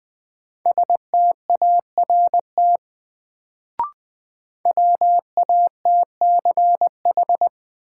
Morse code, heard as a keyed tone, about 20 wpm, 700 Hz. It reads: START WATCH
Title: Sentences from Top 300 Words - New Word Drill - 20wpm